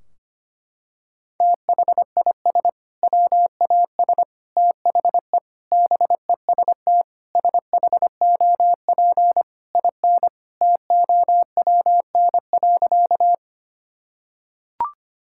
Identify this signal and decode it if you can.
Morse code, heard as a keyed tone, about 25 wpm, 700 Hz. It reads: THIS WAS THE BEST SHOP IN TOWN.
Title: Sentences from Top 1600 Words 25wpm